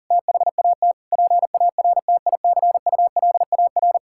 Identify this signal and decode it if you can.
Morse code, heard as a keyed tone, about 40 wpm, 700 Hz. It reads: THAT PARTICULAR
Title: Sentences from Top 1000 Words - Review Part 5 - 40wpm